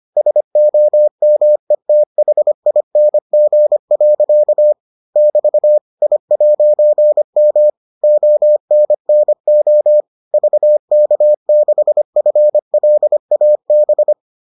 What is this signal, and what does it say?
Morse code, heard as a keyed tone, about 25 wpm, 600 Hz. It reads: SOMETHING. <BT> I'M ONNO VK6FLAB